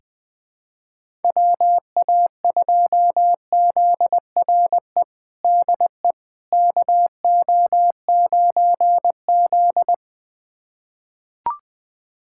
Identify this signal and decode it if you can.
Morse code, heard as a keyed tone, about 21 wpm, 700 Hz. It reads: WA2ZRE DE KO9Z
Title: USA Callsign DE Callsign 20wpm